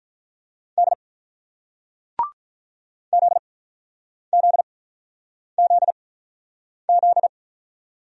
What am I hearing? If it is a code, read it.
Morse code, heard as a keyed tone, about 48 wpm, spaced as for 9 wpm, 700 Hz. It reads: D ZZZZ